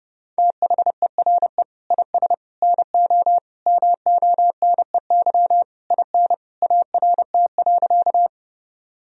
Morse code, heard as a keyed tone, about 30 wpm, 700 Hz. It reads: THERE IS NO MONEY IN ART.
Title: Sentences from Top 1000 Words - Review Part 3 - 30wpm